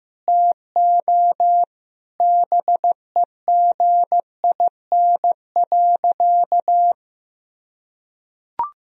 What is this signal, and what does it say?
Morse code, heard as a keyed tone, about 15 wpm, 700 Hz. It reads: TO BEGIN.
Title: Sentences from Top 1000 Words - Review Part 1 - 15wpm